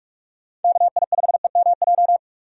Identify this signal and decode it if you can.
Morse code, heard as a keyed tone, about 42 wpm, 700 Hz. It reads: KI5EMJ
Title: USA Call Signs - 6 Characters - 4x Character Spacing - 45wpm